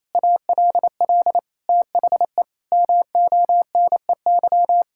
Morse code, heard as a keyed tone, about 28 wpm, 700 Hz. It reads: ALL THE MONEY
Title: Sentences from Top 1000 Words - Review Part 6 - 28wpm